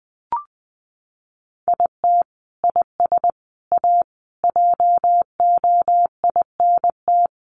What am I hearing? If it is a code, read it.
Morse code, heard as a keyed tone, about 20 wpm, 700 Hz. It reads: IT IS A JOINT
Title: Sentences from Top 1300 Words 20wpm